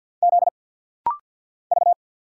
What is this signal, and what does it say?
Morse code, heard as a keyed tone, about 50 wpm, 700 Hz. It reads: Z V
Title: Single Letters - Rapid-Fire 50wpm